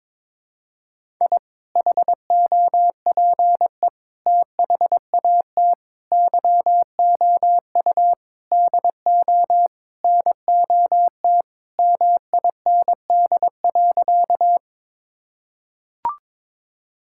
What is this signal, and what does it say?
Morse code, heard as a keyed tone, about 22 wpm, 700 Hz. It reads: I HOPE THAT YOU DO NOT MIND.
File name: practice / 22wpm / Sentences with 4-Word Phrases - 22wpm